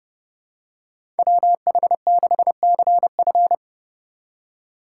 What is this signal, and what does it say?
Morse code, heard as a keyed tone, about 30 wpm, 700 Hz. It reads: WH6CF
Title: USA Call Signs - 3x Character Spacing - 30wpm